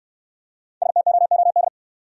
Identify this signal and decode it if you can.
Morse code, heard as a keyed tone, about 65 wpm, 690 Hz. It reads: STOOD